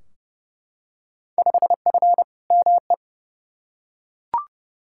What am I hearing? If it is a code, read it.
Morse code, heard as a keyed tone, about 31 wpm, 700 Hz. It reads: 5F ME
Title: Field Day Exchanges Encore - 30wpm